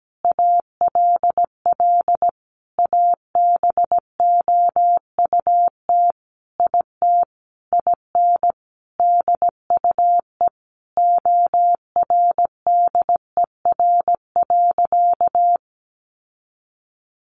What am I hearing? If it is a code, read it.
Morse code, heard as a keyed tone, about 17 wpm, 700 Hz. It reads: ALL ABOUT IT IN DUE ORDER.
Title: Sentences from Top 900 Words 17wpm